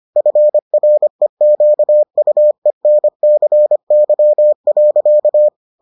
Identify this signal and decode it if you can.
Morse code, heard as a keyed tone, about 25 wpm, 600 Hz. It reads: FREQUENCY.